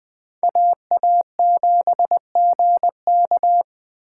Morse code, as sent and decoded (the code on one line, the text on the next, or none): .- .- --... --. -.-
AA7GK